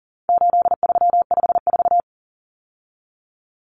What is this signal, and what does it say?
Morse code, heard as a keyed tone, about 40 wpm, 700 Hz. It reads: 8354
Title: Numbers - 4-Digit 40wpm